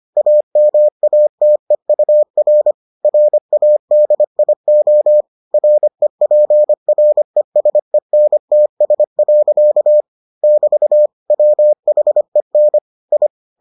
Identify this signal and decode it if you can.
Morse code, heard as a keyed tone, about 25 wpm, 600 Hz. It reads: AMATEUR RADIO REPRESENTS. <BT> WHEN I